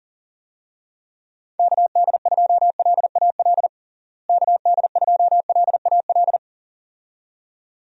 Morse code, heard as a keyed tone, about 37 wpm, 700 Hz. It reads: KD2LAL KD2LAL
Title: USA Call Signs - 6 Characters - Repeated 2x - 40wpm